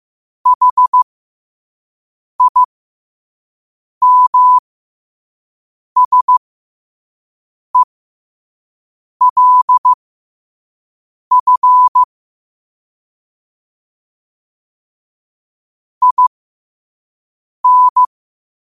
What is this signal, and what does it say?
Morse code, heard as a keyed tone, about 15 wpm, spaced as for 5 wpm, 1000 Hz. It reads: HIMSELF IN
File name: qotdfiles / 5wpm